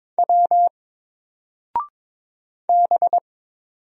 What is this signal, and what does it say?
Morse code, heard as a keyed tone, about 22 wpm, 700 Hz. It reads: W B
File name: Single-Letter-Number-22wpm